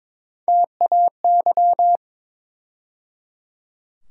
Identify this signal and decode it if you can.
Morse code, heard as a keyed tone, about 21 wpm, 700 Hz. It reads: TAY
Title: USA Call Sign Postfixes - 3 Letters 22wpm